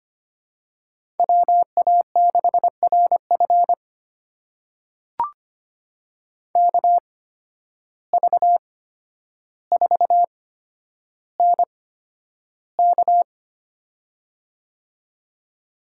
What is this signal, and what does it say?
Morse code, heard as a keyed tone, about 25 wpm, spaced as for 11 wpm, 700 Hz. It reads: WA6RF KV4NK
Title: USA Call Signs - 5 Characters - 8x Character Spacing - 25wpm